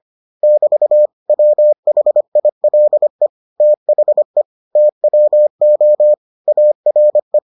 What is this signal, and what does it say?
Morse code, heard as a keyed tone, about 25 wpm, 600 Hz. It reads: <BT> WHILE THE TWO ARE